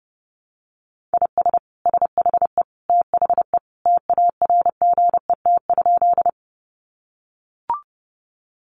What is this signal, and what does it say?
Morse code, heard as a keyed tone, about 30 wpm, 700 Hz. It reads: IS SHE THE TARGET?
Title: Sentences from Top 1200 Words 30wpm